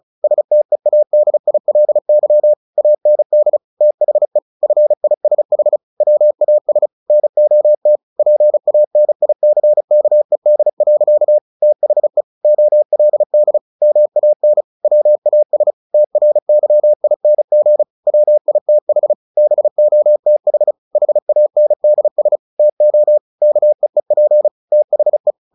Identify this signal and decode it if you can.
Morse code, heard as a keyed tone, about 34 wpm, 600 Hz. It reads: STEADILY AND THE FISH WAS NOT PANICKED. THE OLD MAN WAS TRYING WITH BOTH HANDS TO KEEP THE
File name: the old man and the sea / the old man and the sea 1 35wpm